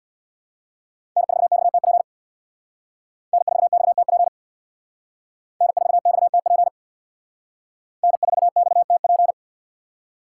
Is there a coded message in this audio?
Yes: Morse code from 1.2 to 9.3 s, heard as a keyed tone, about 63 wpm, 700 Hz. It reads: N4XTP N4XTP N4XTP N4XTP